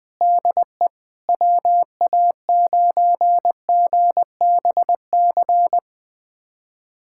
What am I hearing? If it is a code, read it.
Morse code, heard as a keyed tone, about 20 wpm, 700 Hz. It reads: DE WA9GBC
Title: USA Callsign DE Callsign - 6 Characters - 20wpm